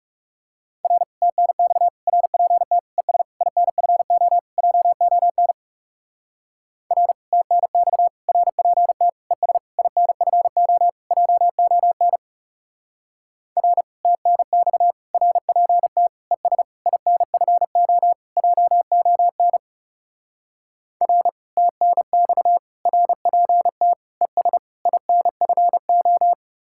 Morse code, heard as a keyed tone, about 36 wpm, 700 Hz. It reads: R TNX RPT ES INFO JON R TNX RPT ES INFO JON R TNX RPT ES INFO JON R TNX RPT ES INFO
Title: Speed-Racing Common QSO Phrases Encore - 30wpm